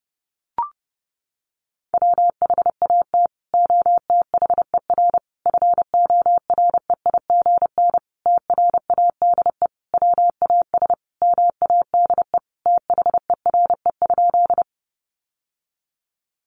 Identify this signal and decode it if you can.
Morse code, heard as a keyed tone, about 30 wpm, 700 Hz. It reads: WHAT OTHER FOREIGN TRADE WAS MADE THERE?
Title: Sentences from Top 1400 Words 30wpm